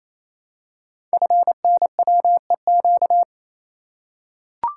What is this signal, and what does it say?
Morse code, heard as a keyed tone, about 28 wpm, 700 Hz. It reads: FNWEQ
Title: Code Groups - 5 Letters - 28wpm